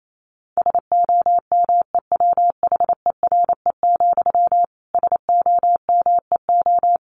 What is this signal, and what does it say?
Morse code, heard as a keyed tone, about 28 wpm, 700 Hz. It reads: SOMEWHERE, SOMEO
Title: A Sound Of Thunder 28wpm